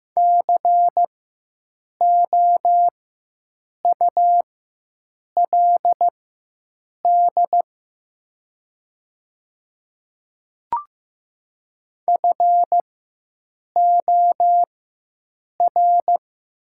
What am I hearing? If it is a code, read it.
Morse code, heard as a keyed tone, about 15 wpm, spaced as for 7 wpm, 700 Hz. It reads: COULD FOR
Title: 4x Character Spacing - Top 100 Words - 15wpm